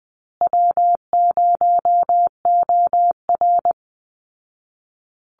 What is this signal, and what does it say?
Morse code, heard as a keyed tone, about 20 wpm, 700 Hz. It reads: W0OR
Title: Easy Call Signs Repeated 3x - 20wpm